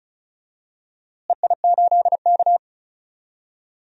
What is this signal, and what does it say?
Morse code, heard as a keyed tone, about 36 wpm, 700 Hz. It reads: EI8K
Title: DXCC-100 Call Signs - 4 Characters - Repeated 3x - 35wpm